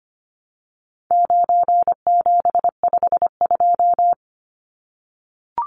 Morse code, heard as a keyed tone, about 25 wpm, 700 Hz. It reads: 9752
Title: Numbers - 4-Digit 25wpm